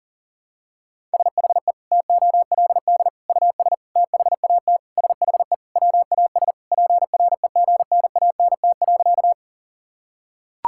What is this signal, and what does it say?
Morse code, heard as a keyed tone, about 40 wpm, 700 Hz. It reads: SHE TOLD US THAT SHE WAS PREGNANT.
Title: Sentences from Top 1000 Words - Review Part 6 - 40wpm